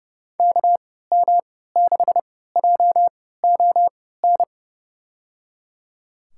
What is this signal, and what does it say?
Morse code, heard as a keyed tone, about 30 wpm, spaced as for 17 wpm, 700 Hz. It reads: KM6JON
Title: USA Call Signs - 6 Characters - 3x Character Spacing - 30wpm